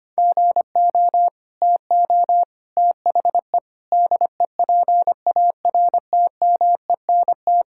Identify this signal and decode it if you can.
Morse code, heard as a keyed tone, about 25 wpm, 700 Hz. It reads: GO TO THE DEPARTMENT
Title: Sentences from Top 1000 Words - Review Part 5 - 25wpm